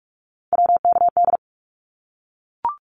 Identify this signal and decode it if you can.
Morse code, heard as a keyed tone, about 44 wpm, 700 Hz. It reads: WKD
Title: Abbreviations Repeated 3x - 45wpm